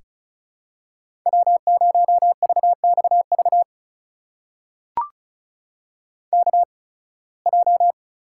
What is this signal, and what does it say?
Morse code, heard as a keyed tone, about 36 wpm, spaced as for 23 wpm, 700 Hz. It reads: W0VXV KJ
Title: USA Call Signs - 8x Character Spacing - 35wpm